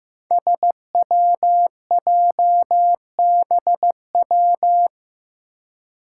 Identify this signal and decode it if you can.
Morse code, heard as a keyed tone, about 15 wpm, 700 Hz. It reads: SWJBW
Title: Five Challenging Letter-Number Combinations 15wpm